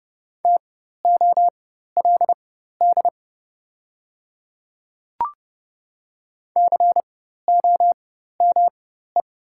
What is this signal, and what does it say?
Morse code, heard as a keyed tone, about 30 wpm, spaced as for 14 wpm, 700 Hz. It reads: TOLD COME
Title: 4x Character Spacing - Top 500 Words - 30wpm